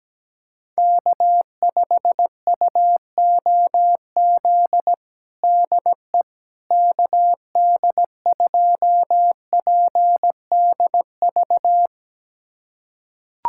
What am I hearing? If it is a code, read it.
Morse code, heard as a keyed tone, about 17 wpm, 700 Hz. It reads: K5UOZ DE KD2PDV
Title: USA Callsign DE Callsign 17wpm